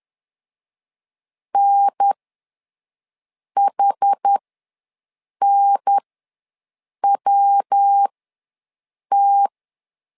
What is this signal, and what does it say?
Morse code, heard as a keyed tone, about 10 wpm, spaced as for 6 wpm, 800 Hz. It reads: NHNWT
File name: LEZIONI / 3 MWRO / 3 MW ws 10wpm 2213